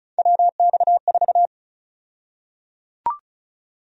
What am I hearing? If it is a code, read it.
Morse code, heard as a keyed tone, about 34 wpm, 700 Hz. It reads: WX4
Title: USA Call Sign Prefixes - Repeated 2x - 35wpm